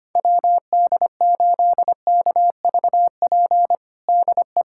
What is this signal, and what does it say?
Morse code, heard as a keyed tone, about 25 wpm, 700 Hz. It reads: WD8KVP DE
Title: USA Callsign DE Callsign - 6 Characters - 25wpm